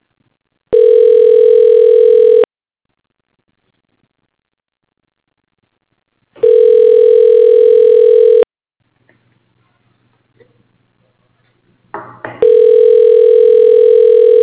Ringback tone (North America), 440 + 480 Hz.
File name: recordings / RE0ec b da5